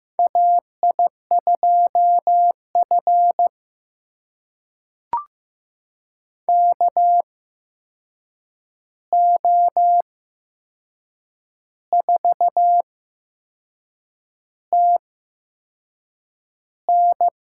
Morse code, heard as a keyed tone, about 15 wpm, spaced as for 6 wpm, 700 Hz. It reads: AI2F KO4TN
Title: USA Call Signs - 8x Character Spacing - 15wpm